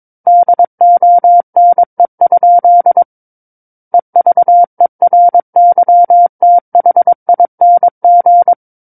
Morse code, heard as a keyed tone, about 22 wpm, 700 Hz. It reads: DONE? EVERYTHING
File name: Book One / 13-Chapter 13 - 22 WPM